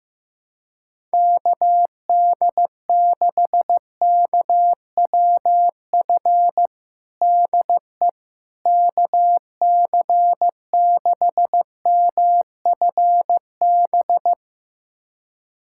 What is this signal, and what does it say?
Morse code, heard as a keyed tone, about 16 wpm, 700 Hz. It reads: KD6KWF DE KC6MFB
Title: USA Callsign DE Callsign - 6 Characters - 15wpm